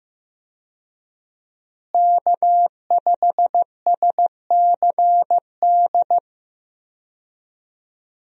Morse code, heard as a keyed tone, about 16 wpm, 700 Hz. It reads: K5SCD
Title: USA Call Signs - 5 Characters - Repeated 2x - 15wpm